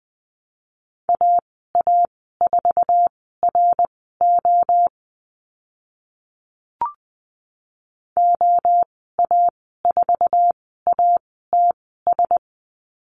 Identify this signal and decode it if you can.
Morse code, heard as a keyed tone, about 20 wpm, spaced as for 14 wpm, 700 Hz. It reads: AA4RO OA4ATS
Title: 062 Lesson 11g - Letter h - Call Signs 20wpm